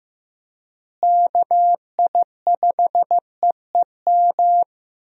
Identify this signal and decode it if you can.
Morse code, heard as a keyed tone, about 15 wpm, 700 Hz. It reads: KI5EEM
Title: USA Call Signs - 6 Characters - Repeated 2x - 15wpm